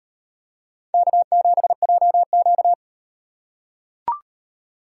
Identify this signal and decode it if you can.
Morse code, heard as a keyed tone, about 38 wpm, 700 Hz. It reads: K7JQ